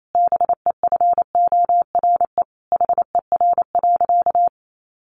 Morse code, heard as a keyed tone, about 28 wpm, 700 Hz. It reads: BEFORE HER.
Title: Sentences from Top 1400 Words 28wpm